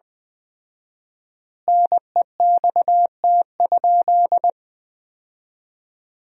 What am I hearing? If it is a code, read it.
Morse code, heard as a keyed tone, about 20 wpm, 700 Hz. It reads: NEXT?
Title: 4x Word-Spacing - Sentences from Top 300 Words - 20wpm